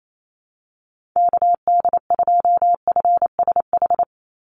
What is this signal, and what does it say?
Morse code, heard as a keyed tone, about 28 wpm, 700 Hz. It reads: KD2FSH
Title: Call Signs Repeated 2x - 28wpm